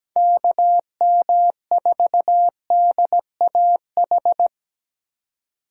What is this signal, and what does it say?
Morse code, heard as a keyed tone, about 20 wpm, 700 Hz. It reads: KM4DAH
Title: Speed Racing - USA 2x3 Call Signs 17wpm